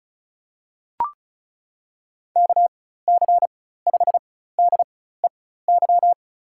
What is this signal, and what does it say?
Morse code, heard as a keyed tone, about 36 wpm, spaced as for 16 wpm, 700 Hz. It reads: KC5DEY
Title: USA Call Signs - 6 Characters - 4x Character Spacing - 35wpm